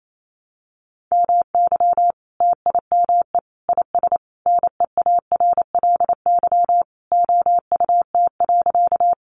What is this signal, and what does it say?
Morse code, heard as a keyed tone, about 28 wpm, 700 Hz. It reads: MY TIME IS NEARLY OUT.
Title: Sentences from Top 1300 Words 28wpm